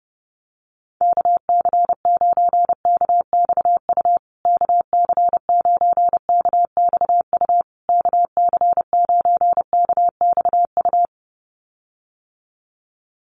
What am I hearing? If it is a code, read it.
Morse code, heard as a keyed tone, about 28 wpm, 700 Hz. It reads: KC9KXU KC9KXU KC9KXU
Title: USA Call Signs Repeated 3x 30wpm